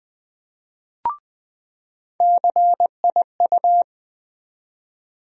Morse code, heard as a keyed tone, about 20 wpm, 700 Hz. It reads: CIU
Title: USA Call Sign Postfixes - 3 Letters 20wpm